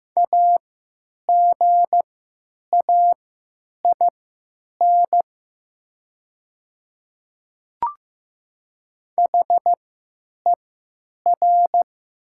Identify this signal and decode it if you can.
Morse code, heard as a keyed tone, about 15 wpm, spaced as for 9 wpm, 700 Hz. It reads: AGAIN HER
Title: 3x Character Spacing - Top 500 Words - 15wpm